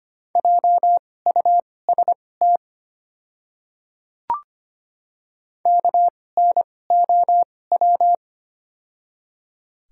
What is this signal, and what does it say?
Morse code, heard as a keyed tone, about 25 wpm, spaced as for 18 wpm, 700 Hz. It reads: JUST KNOW
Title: 2x Character Spacing - Top 200 Words - 25wpm